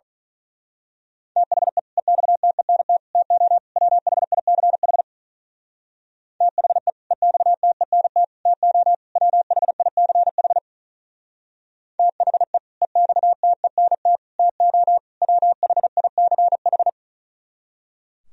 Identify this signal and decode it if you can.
Morse code, heard as a keyed tone, about 42 wpm, 700 Hz. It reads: THE EXTENT TO WHICH THE EXTENT TO WHICH THE EXTENT TO WHICH
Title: Speed-Racing - Sets of 4 Words 35wpm